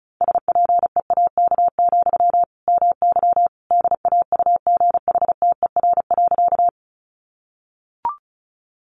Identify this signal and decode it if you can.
Morse code, heard as a keyed tone, about 34 wpm, 700 Hz. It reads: SPEAK, MY DAUGHTER.
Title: Sentences from Top 1500 Words 35wpm